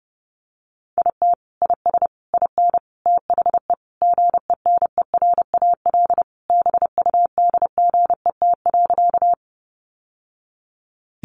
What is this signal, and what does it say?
Morse code, heard as a keyed tone, about 28 wpm, 700 Hz. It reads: IT IS IN THE GENERAL BUDGET.
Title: Sentences from Top 1200 Words 30wpm